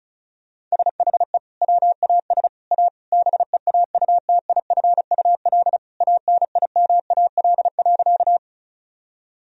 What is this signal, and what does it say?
Morse code, heard as a keyed tone, about 34 wpm, 700 Hz. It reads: SHE WAS A BEAUTIFUL ANIMAL.
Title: Sentences from Top 1000 Words - Review Part 1 - 35wpm